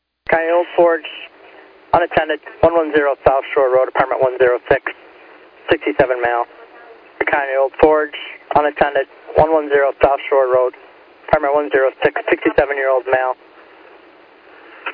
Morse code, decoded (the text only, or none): none